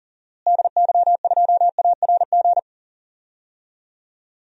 Morse code, heard as a keyed tone, about 40 wpm, 700 Hz. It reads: DY2ARG
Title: DXCC-100 Call Signs - 6 Characters - Repeated 3x - 40wpm